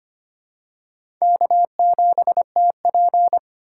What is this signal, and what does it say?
Morse code, heard as a keyed tone, about 25 wpm, 700 Hz. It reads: K7TP
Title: USA Call Signs - 4 Characters - 2x Character Spacing - 25wpm